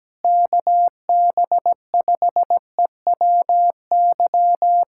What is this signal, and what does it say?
Morse code, heard as a keyed tone, about 17 wpm, 700 Hz. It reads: KB5EWY
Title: USA Callsign DE Callsign - 6 Characters - 17wpm